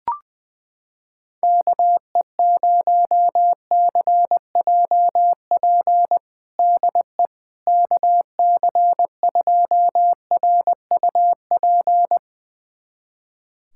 Morse code, heard as a keyed tone, about 20 wpm, 700 Hz. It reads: KE0CJP DE KC2RUP
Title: USA Callsign DE Callsign - 6 Characters - 20wpm